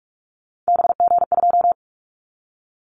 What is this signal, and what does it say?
Morse code, heard as a keyed tone, about 46 wpm, 700 Hz. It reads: BG2
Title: Callsign Prefixes for Top 100 DXCC - 45wpm